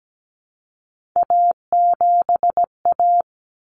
Morse code, heard as a keyed tone, about 17 wpm, 700 Hz. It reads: A7A
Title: Common Cut Numbers - 3-Digit 17wpm